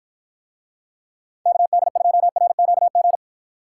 Morse code, heard as a keyed tone, about 53 wpm, 700 Hz. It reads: KD2RQG